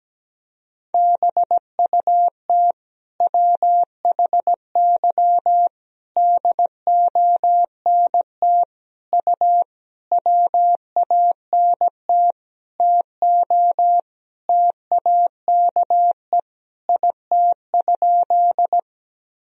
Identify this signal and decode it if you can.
Morse code, heard as a keyed tone, about 17 wpm, 700 Hz. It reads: BUT WHY DONT U WANT TO TAKE IT?
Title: Sentences from Top 300 Words and Abbreviations 17wpm